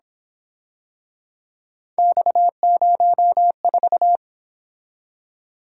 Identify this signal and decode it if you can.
Morse code, heard as a keyed tone, about 27 wpm, 700 Hz. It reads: X04
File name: Speed-Racing - Three Letter-Number Combinations - 22wpm